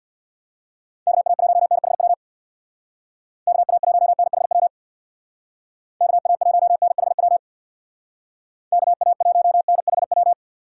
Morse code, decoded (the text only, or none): KA1NHW KA1NHW KA1NHW KA1NHW